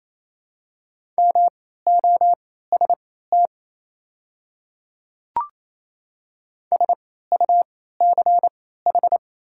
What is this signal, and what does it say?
Morse code, heard as a keyed tone, about 28 wpm, spaced as for 16 wpm, 700 Hz. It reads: MOST SUCH
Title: 3x Character Spacing - Top 200 Words - 28wpm